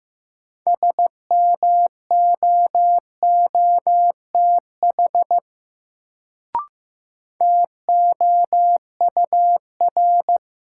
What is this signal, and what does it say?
Morse code, heard as a keyed tone, about 15 wpm, 700 Hz. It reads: SMOOTH TOUR
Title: Sentences from Top 1400 Words - New Word Drill Alt - 15wpm